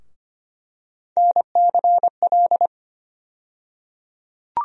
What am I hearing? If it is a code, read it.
Morse code, heard as a keyed tone, about 25 wpm, 700 Hz. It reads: NCL
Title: USA Call Sign Postfixes - 3 Letters 25wpm